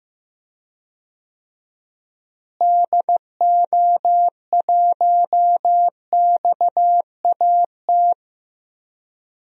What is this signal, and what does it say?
Morse code, heard as a keyed tone, about 15 wpm, 700 Hz. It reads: DO1XAT
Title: DXCC-100 Call Signs - 6 Characters - Repeated 2x - 15wpm